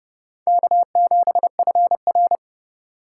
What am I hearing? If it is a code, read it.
Morse code, heard as a keyed tone, about 26 wpm, 700 Hz. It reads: K7FR